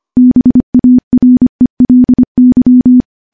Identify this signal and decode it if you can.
Morse code, heard as a keyed tone, about 25 wpm, 250 Hz. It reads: BARELY
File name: morse-training-25WPM